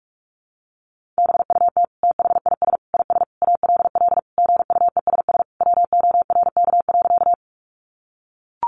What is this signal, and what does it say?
Morse code, heard as a keyed tone, about 44 wpm, 700 Hz. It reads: BUT THIS IS ALL GUESS WORK.
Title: Sentences from Top 1500 Words 45wpm